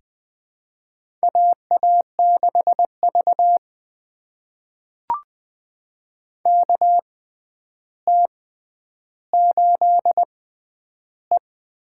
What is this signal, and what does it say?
Morse code, heard as a keyed tone, about 20 wpm, spaced as for 10 wpm, 700 Hz. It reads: AA6V KT8E
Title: USA Call Signs - 4 Characters - 6x Character Spacing - 20wpm